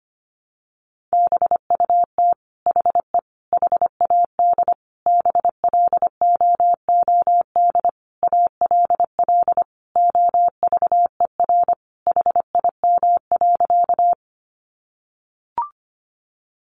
Morse code, heard as a keyed tone, about 25 wpm, 700 Hz. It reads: BUT HE HAD BLOOD ALL OVER HIM.
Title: Sentences from Top 1100 Words 25wpm